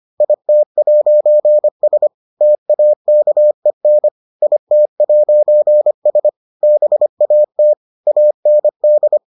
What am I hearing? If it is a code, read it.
Morse code, heard as a keyed tone, about 25 wpm, 600 Hz. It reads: IT'S TAKEN IT'S BAT AND